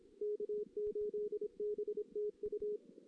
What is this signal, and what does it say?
Morse code, heard as a keyed tone, about 28 wpm, 440 Hz. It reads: K8BTU
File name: 2016 JAN CW NAQP / 2016 Jan naqp k5ot - 0-2